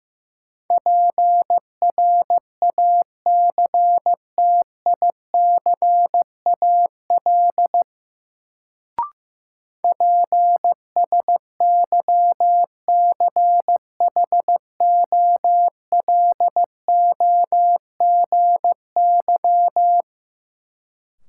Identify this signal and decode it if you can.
Morse code, heard as a keyed tone, about 15 wpm, 700 Hz. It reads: PRACTICAL PSYCHOLOGY
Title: Sentences from Top 1400 Words - New Word Drill - 15wpm